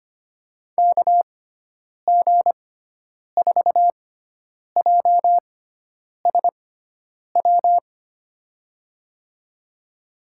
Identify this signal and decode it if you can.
Morse code, heard as a keyed tone, about 25 wpm, spaced as for 9 wpm, 700 Hz. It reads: KG4JSW
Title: USA Call Signs - 6x Character Spacing - 25wpm